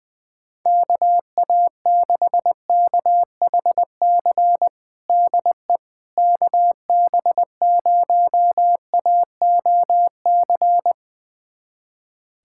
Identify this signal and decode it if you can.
Morse code, heard as a keyed tone, about 21 wpm, 700 Hz. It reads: KA6KHC DE KB0AOC
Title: USA Callsign DE Callsign - 6 Characters - 20wpm